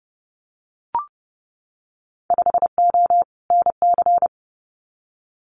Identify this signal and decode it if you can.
Morse code, heard as a keyed tone, about 30 wpm, 700 Hz. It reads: 5O NC
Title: Winter Field Day Exchanges 30wpm